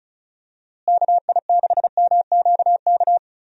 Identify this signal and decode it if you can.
Morse code, heard as a keyed tone, about 36 wpm, 700 Hz. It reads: KI6MQK
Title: USA Call Signs - 6 Characters - Repeated 3x - 35wpm